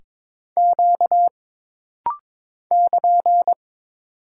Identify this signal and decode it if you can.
Morse code, heard as a keyed tone, about 22 wpm, 700 Hz. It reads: Q <KN>
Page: This rapid-fire timing is geared to push your ICR (Instant Character Recognition) skills.